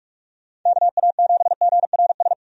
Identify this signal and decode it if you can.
Morse code, heard as a keyed tone, about 46 wpm, 700 Hz. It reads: KA7GRS
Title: USA Call Signs - 6 Characters - 45wpm